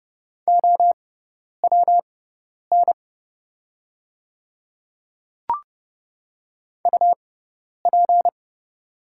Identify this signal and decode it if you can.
Morse code, heard as a keyed tone, about 30 wpm, spaced as for 10 wpm, 690 Hz. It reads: OWN UP
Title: 6x Character Spacing - Top 300 Words - 30wpm